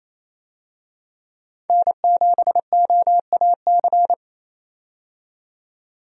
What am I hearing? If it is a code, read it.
Morse code, heard as a keyed tone, about 28 wpm, 700 Hz. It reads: N7OAC
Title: USA Call Signs - 5 Characters - 28wpm